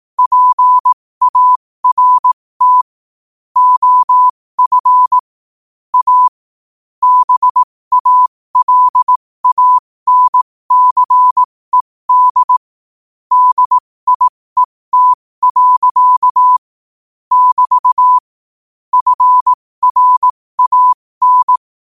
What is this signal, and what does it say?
Morse code, heard as a keyed tone, about 18 wpm, spaced as for 15 wpm, 1000 Hz. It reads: PART OF A BALANCED DIET. <BT> FRAN
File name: qotdfiles / 15wpm